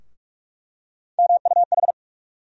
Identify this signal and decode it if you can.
Morse code, heard as a keyed tone, about 42 wpm, 700 Hz. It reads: MUH
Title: USA Call Sign Postfixes - 3 Letters 45wpm